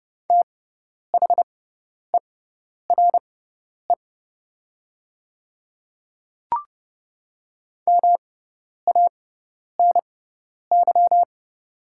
Morse code, heard as a keyed tone, about 30 wpm, spaced as for 10 wpm, 700 Hz. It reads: THERE MANY
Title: 6x Character Spacing - Top 100 Words - 30wpm